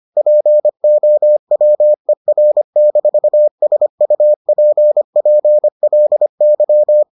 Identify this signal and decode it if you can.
Morse code, heard as a keyed tone, about 25 wpm, 600 Hz. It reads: POWER-SUPPLY